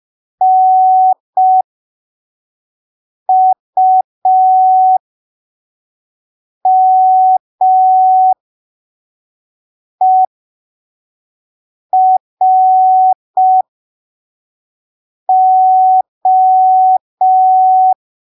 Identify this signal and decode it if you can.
Morse code, heard as a keyed tone, about 5 wpm, spaced as for 3 wpm, 750 Hz. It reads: NUMERO
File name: arquivos / texto / texto 22 05ppm